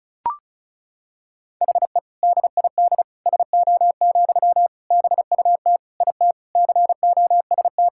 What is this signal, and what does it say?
Morse code, heard as a keyed tone, about 36 wpm, 700 Hz. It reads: HE DID SO, BUT IT COST
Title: Sentences from Top 1000 Words - Review Part 5 - 35wpm